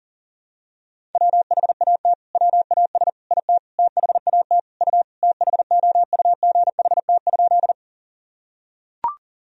Morse code, heard as a keyed tone, about 40 wpm, 700 Hz. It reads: WHAT WAS IT THAT U THOUGHT?
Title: Sentences from Top 300 Words and Abbreviations 40wpm